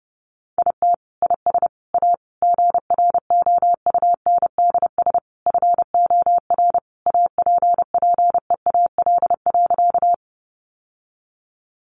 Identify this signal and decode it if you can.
Morse code, heard as a keyed tone, about 30 wpm, 700 Hz. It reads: IT IS A GROUNDS FOR APPEAL.
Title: Sentences from Top 1300 Words 30wpm